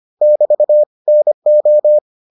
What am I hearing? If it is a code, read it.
Morse code, heard as a keyed tone, about 25 wpm, 600 Hz. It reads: <BT> NO